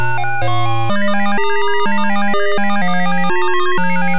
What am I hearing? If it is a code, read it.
There is no code.